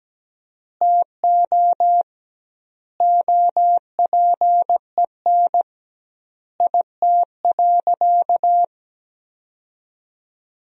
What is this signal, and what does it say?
Morse code, heard as a keyed tone, about 17 wpm, 700 Hz. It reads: TO OPEN IT.
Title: 2x Word-Spacing - Sentences from Top 300 Words - 17wpm